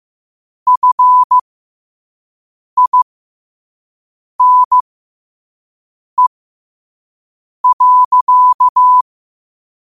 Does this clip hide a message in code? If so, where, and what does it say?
Yes: Morse code from 0.7 to 9.0 s, heard as a keyed tone, about 15 wpm, spaced as for 5 wpm, 1000 Hz. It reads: FINE.